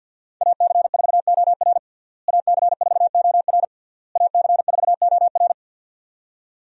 Morse code, heard as a keyed tone, about 52 wpm, 690 Hz. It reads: AC4OR AC4OR AC4OR
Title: USA Call Signs - 5 Characters - Repeated 3x - 50wpm